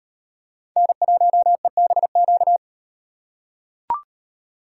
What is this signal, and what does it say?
Morse code, heard as a keyed tone, about 38 wpm, 700 Hz. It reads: N1EBQ